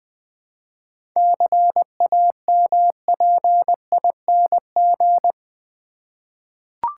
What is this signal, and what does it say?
Morse code, heard as a keyed tone, about 20 wpm, 700 Hz. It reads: CAMPING
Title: 7 Letter Words - First 3 Letters Given - 20wpm